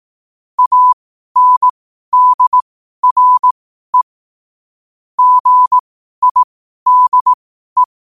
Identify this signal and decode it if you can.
Morse code, heard as a keyed tone, about 18 wpm, spaced as for 12 wpm, 1000 Hz. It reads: ANDRE GIDE
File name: qotdfiles / 12wpm